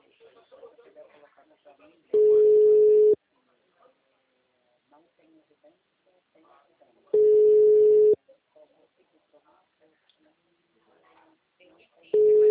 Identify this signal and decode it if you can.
Ringback tone (Europe, 425 Hz), with speech.
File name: out- -2011- - - .57660